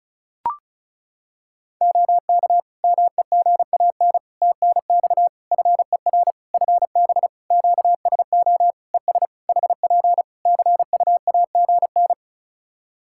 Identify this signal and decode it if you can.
Morse code, heard as a keyed tone, about 34 wpm, 700 Hz. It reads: OK MEGAN TNX FER FB QSO ES HP CUAGN
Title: Practice QSO - Exchange 4 - Ending - 35wpm